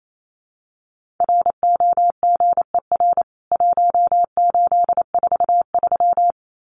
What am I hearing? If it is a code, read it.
Morse code, heard as a keyed tone, about 28 wpm, 700 Hz. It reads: ROGER 1843
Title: CWT Exchange 28wpm